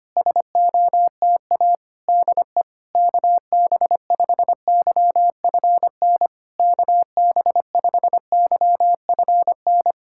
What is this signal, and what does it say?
Morse code, heard as a keyed tone, about 25 wpm, 660 Hz. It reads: SOTA DE KB5YFN KB5YFN
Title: SOTA QSO Practice 25wpm